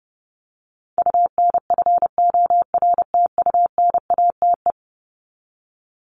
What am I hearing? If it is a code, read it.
Morse code, heard as a keyed tone, about 30 wpm, 700 Hz. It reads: UNFORTUNATE
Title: LY Suffix - Top 100 Words with Base Words 30wpm